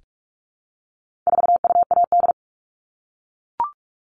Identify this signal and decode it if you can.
Morse code, heard as a keyed tone, about 44 wpm, 700 Hz. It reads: 4UAD
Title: All Cut Numbers - 4-Digit 45wpm